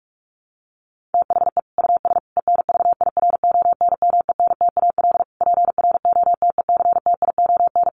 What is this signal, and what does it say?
Morse code, heard as a keyed tone, about 44 wpm, 700 Hz. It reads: THE US ENVIRONMENTAL PROTECTION
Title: Sets of 5 Words Encore2 45wpm